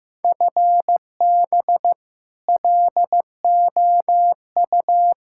Morse code, heard as a keyed tone, about 16 wpm, 700 Hz. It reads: FB LOU
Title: K1USN SST QSO Practice 15wpm